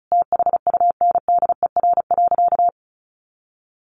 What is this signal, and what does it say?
Morse code, heard as a keyed tone, about 34 wpm, 700 Hz. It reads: THUNDER.
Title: A Sound Of Thunder 35wpm